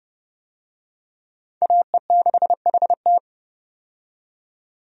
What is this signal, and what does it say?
Morse code, heard as a keyed tone, about 30 wpm, 700 Hz. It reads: AE6HT
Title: USA Call Signs - 5 Characters - 30wpm